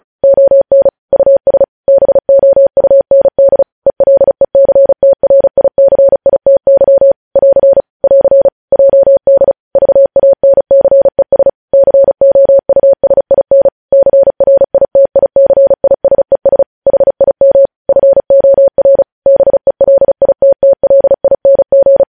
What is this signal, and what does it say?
Morse code, heard as a keyed tone, about 36 wpm, 550 Hz. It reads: ON US BOUND ELECTRICITY <AR> <AR> JD VANCES COUSIN CRITICISES HIM FOR BELITTLING
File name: podcast / News headlines-1x-35x35